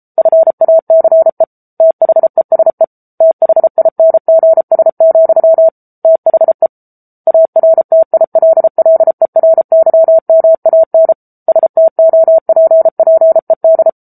Morse code, heard as a keyed tone, about 34 wpm, 650 Hz. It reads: FACE THESE THINGS, THE ARTILLERYMAN STOPPED
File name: Book Two / 07-Chapter 7 - 33 WPM